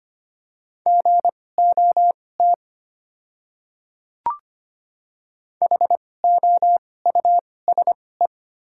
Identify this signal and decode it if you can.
Morse code, heard as a keyed tone, about 25 wpm, spaced as for 18 wpm, 700 Hz. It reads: GOT HOUSE
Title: 2x Character Spacing - Top 300 Words - 25wpm